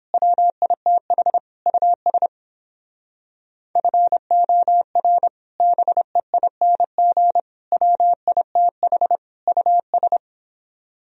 Morse code, heard as a keyed tone, about 29 wpm, 700 Hz. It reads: WITH US FOR BEING WITH US